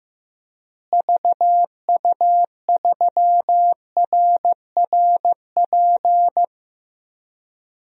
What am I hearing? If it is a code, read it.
Morse code, heard as a keyed tone, about 15 wpm, 700 Hz. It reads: VU3RRP
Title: DXCC-100 Call Signs - 6 Characters - Repeated 2x - 15wpm